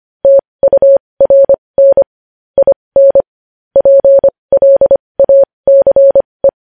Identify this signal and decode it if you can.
Morse code, heard as a keyed tone, about 25 wpm, spaced as for 20 wpm, 550 Hz. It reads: TURN IN PLACE
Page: news-headlines-in-morse-code-at-20-wpm